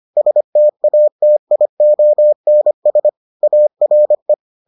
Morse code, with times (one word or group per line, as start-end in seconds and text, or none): 0.2-3.1 STATIONS
3.4-4.3 ARE